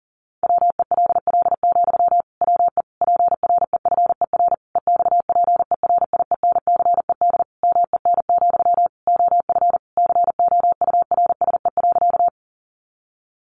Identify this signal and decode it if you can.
Morse code, heard as a keyed tone, about 40 wpm, 700 Hz. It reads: WELL, WE PREFER EXPERIENCED MEN, OF COURSE.
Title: Sentences from Top 1100 Words 40wpm